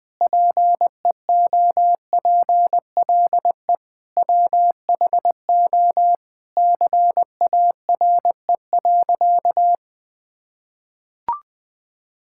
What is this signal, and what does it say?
Morse code, heard as a keyed tone, about 20 wpm, 700 Hz. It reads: PEOPLE WHO CARE.
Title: Sentences with 5-Word Phrases - 20wpm